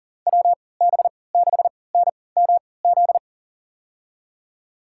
Morse code, heard as a keyed tone, about 40 wpm, spaced as for 23 wpm, 700 Hz. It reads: WB6NMZ
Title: USA Call Signs - 6 Characters - 3x Character Spacing - 40wpm